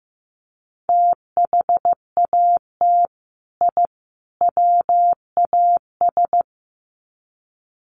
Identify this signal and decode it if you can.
Morse code, heard as a keyed tone, about 15 wpm, 700 Hz. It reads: THAT I WAS
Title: Sets of 3 Words 15wpm